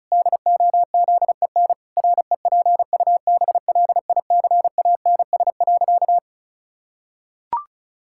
Morse code, heard as a keyed tone, about 36 wpm, 700 Hz. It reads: DOZEN REPUBLICANS.
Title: Pangram Challenge v2 - 35wpm